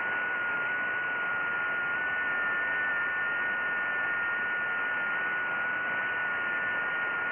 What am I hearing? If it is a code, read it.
FT8.
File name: IC-7100